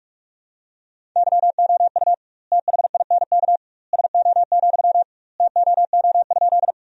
Morse code, heard as a keyed tone, about 46 wpm, 700 Hz. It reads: YOU THINK SO, TOO?
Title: 2x Word-Spacing - Sentences from Top 200 Words - 45wpm